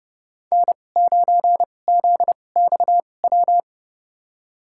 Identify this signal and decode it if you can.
Morse code, heard as a keyed tone, about 31 wpm, spaced as for 22 wpm, 700 Hz. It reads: N9ZXW
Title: USA Call Signs - 5 Characters - 2x Character Spacing - 30wpm